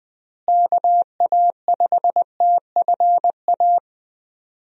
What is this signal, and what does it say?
Morse code, heard as a keyed tone, about 20 wpm, 700 Hz. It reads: KA5TFA
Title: USA Call Signs - 6 Characters - Repeated 2x - 20wpm